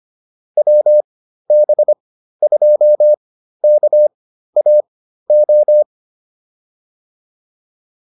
Morse code, heard as a keyed tone, about 25 wpm, spaced as for 13 wpm, 600 Hz. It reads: WB2KAO